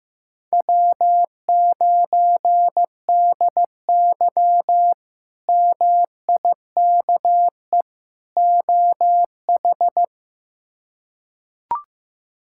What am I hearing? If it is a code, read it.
Morse code, heard as a keyed tone, about 16 wpm, 700 Hz. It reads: W9DY MIKE OH
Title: K1USN SST QSO Practice - 15wpm